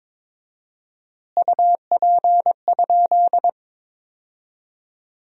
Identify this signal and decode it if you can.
Morse code, heard as a keyed tone, about 22 wpm, 700 Hz. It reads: UP?
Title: 4x Word-Spacing - Sentences from Top 500 Words - 22wpm